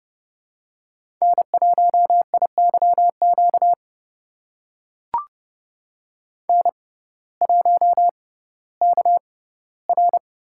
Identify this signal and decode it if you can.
Morse code, heard as a keyed tone, about 31 wpm, spaced as for 17 wpm, 700 Hz. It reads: N1IYQ N1KR
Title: USA Call Signs - 6x Character Spacing - 30wpm